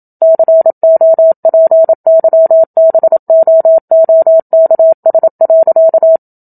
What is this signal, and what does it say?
Morse code, heard as a keyed tone, about 27 wpm, 650 Hz. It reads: COPYBOOKS.
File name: Book Two / 01-Chapter 1 - 27 WPM